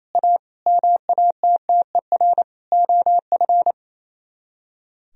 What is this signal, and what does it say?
Morse code, heard as a keyed tone, about 33 wpm, 700 Hz. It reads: A MATTER OF
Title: Speed-Racing - Sets of 4 Words 28wpm